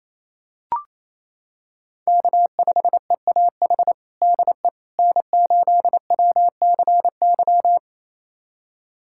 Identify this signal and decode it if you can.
Morse code, heard as a keyed tone, about 28 wpm, 700 Hz. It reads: K5EAH DE N8WCY
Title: USA Callsign DE Callsign - 5 Characters - 28wpm